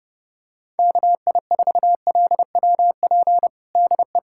Morse code, heard as a keyed tone, about 30 wpm, 690 Hz. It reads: KI4LWP DE
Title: USA Callsign DE Callsign 30wpm